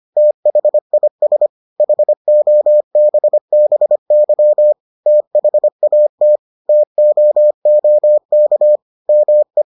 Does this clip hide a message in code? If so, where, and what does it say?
Yes: Morse code from 0.2 to 9.6 s, heard as a keyed tone, about 25 wpm, 600 Hz. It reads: THIS HOBBY THAT TOOK ME